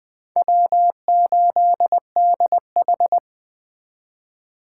Morse code, heard as a keyed tone, about 20 wpm, 700 Hz. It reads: W8DH
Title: Speed Racing - USA Call Signs 17wpm